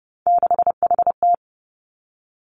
Morse code, heard as a keyed tone, about 30 wpm, 700 Hz. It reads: 6HT
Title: Three Challenging Letter-Number Combinations 30wpm